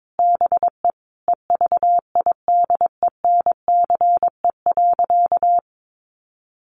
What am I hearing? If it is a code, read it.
Morse code, heard as a keyed tone, about 22 wpm, 700 Hz. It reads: BE EVIDENCE.
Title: Sentences from Top 1300 Words 22wpm